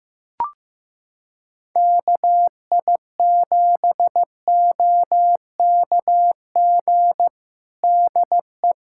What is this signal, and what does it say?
Morse code, heard as a keyed tone, about 15 wpm, 700 Hz. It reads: KI7OKG DE
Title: Practice QSO - Exchange 4 - Ending - 15wpm